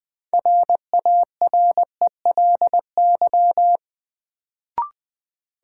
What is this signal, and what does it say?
Morse code, heard as a keyed tone, about 21 wpm, 700 Hz. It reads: RARELY
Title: Sentences from Top 1400 Words - New Word Drill - 20wpm